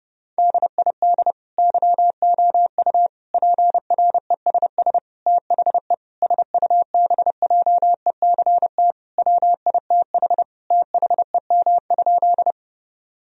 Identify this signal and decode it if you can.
Morse code, heard as a keyed tone, about 32 wpm, 700 Hz. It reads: DID YOU PRESS THE SUBJECT WITH THEM?
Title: Sentences from Top 1000 Words - Review Part 1 - 30wpm